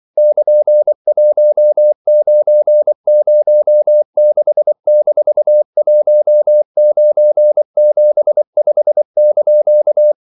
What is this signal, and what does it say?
Morse code, heard as a keyed tone, about 24 wpm, 600 Hz. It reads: <KN>1906-1975)
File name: CW / Practice / EnglishQuotations0324